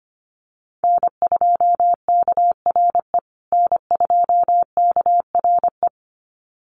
Morse code, heard as a keyed tone, about 25 wpm, 700 Hz. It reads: N2KRE N2KRE
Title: USA Call Signs Repeated 2x 25wpm